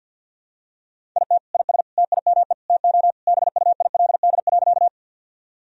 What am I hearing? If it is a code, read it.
Morse code, heard as a keyed tone, about 47 wpm, 700 Hz. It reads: IT IS TIME TO BUILD.
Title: Sentences from Top 1000 Words - Review Part 1 - 50wpm